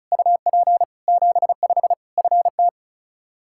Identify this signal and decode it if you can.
Morse code, heard as a keyed tone, about 34 wpm, 700 Hz. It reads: UP 75 FT
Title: K1USN SST QSO Practice 35wpm